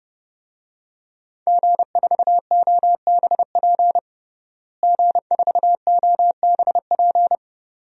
Morse code, heard as a keyed tone, about 30 wpm, 700 Hz. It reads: G4OBP G4OBP
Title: DXCC-100 Call Signs - 5 Characters - Repeated 2x - 30wpm